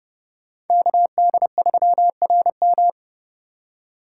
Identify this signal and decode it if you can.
Morse code, heard as a keyed tone, about 29 wpm, 700 Hz. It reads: KD3RM